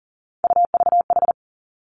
Morse code, heard as a keyed tone, about 40 wpm, 700 Hz. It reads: UVH
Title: Three Challenging Letter-Number Combinations 40wpm